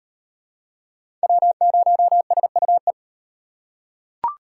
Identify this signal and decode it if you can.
Morse code, heard as a keyed tone, about 35 wpm, 700 Hz. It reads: W0SUE